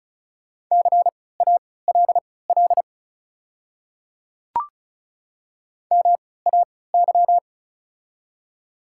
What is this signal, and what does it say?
Morse code, heard as a keyed tone, about 36 wpm, spaced as for 20 wpm, 700 Hz. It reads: CALL MAY
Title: 3x Character Spacing - Top 300 Words - 35wpm